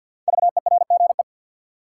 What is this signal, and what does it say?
Morse code, heard as a keyed tone, about 53 wpm, 700 Hz. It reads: VERGE
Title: Sets of 4 Words - New Word Drill - 50wpm